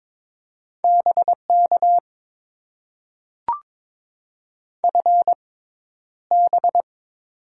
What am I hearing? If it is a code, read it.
Morse code, heard as a keyed tone, about 22 wpm, spaced as for 11 wpm, 700 Hz. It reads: BK FB
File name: Top 90 QSO Elements - 6x Character Spacing - 22wpm